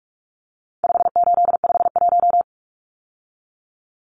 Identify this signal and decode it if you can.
Morse code, heard as a keyed tone, about 44 wpm, 700 Hz. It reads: 5851
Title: Numbers - 4-Digit 45wpm